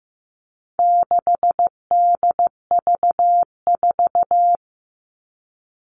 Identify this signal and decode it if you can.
Morse code, heard as a keyed tone, about 15 wpm, 700 Hz. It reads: 6DV4
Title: All Cut Numbers - 4-Digit 15wpm